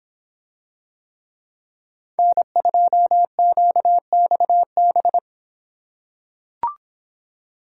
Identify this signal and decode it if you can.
Morse code, heard as a keyed tone, about 26 wpm, 700 Hz. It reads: N2QXB